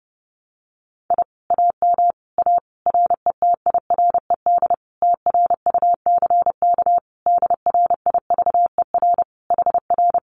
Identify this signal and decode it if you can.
Morse code, heard as a keyed tone, about 30 wpm, 700 Hz. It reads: I AM A RETIRED TRUCK DRIVER HR
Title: Common QSO Phrases Encore 30wpm